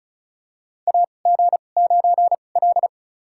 Morse code, heard as a keyed tone, about 36 wpm, spaced as for 25 wpm, 700 Hz. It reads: AG9L
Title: USA Call Signs - 4 Characters - 2x Character Spacing - 35wpm